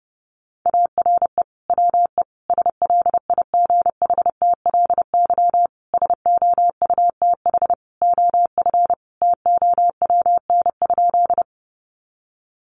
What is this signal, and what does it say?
Morse code, heard as a keyed tone, about 30 wpm, 700 Hz. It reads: ARE WE SLIGHTLY SOUTH OF TOWN?
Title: Sentences from Top 1300 Words 30wpm